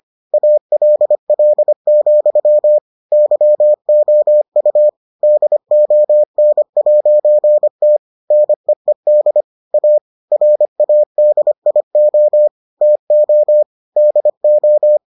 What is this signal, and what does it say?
Morse code, heard as a keyed tone, about 25 wpm, 600 Hz. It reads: ALL, YOU DON'T NEED A RADIO TO DO